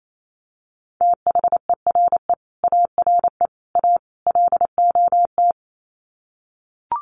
Morse code, heard as a keyed tone, about 28 wpm, 700 Hz. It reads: THERE ARE A LOT
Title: Sets of 4 Words 28wpm